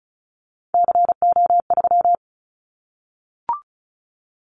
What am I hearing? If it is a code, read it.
Morse code, heard as a keyed tone, about 36 wpm, 700 Hz. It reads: CO3
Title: Callsign Prefixes for Top 100 DXCC - 35wpm